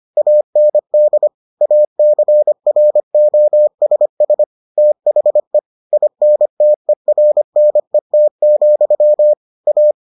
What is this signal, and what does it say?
Morse code, heard as a keyed tone, about 25 wpm, 600 Hz. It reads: AND ACROSS THE INTERNET, A